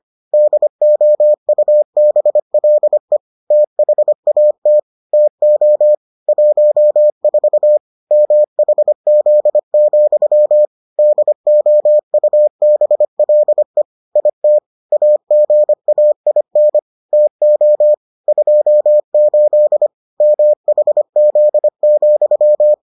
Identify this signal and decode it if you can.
Morse code, heard as a keyed tone, about 25 wpm, 600 Hz. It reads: DOUBLE THAT TO 14 MHZ, DOUBLE IT AGAIN TO 28 MHZ,